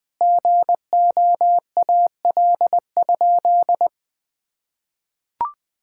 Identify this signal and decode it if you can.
Morse code, heard as a keyed tone, about 20 wpm, 700 Hz. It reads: GOAL?
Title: Sentences from Top 1000 Words - Review Part 6 - 20wpm